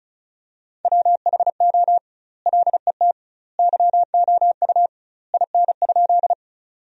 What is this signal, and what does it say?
Morse code, heard as a keyed tone, about 36 wpm, 700 Hz. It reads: WHO LET YOU IN?
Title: 2x Word-Spacing - Sentences from Top 300 Words - 35wpm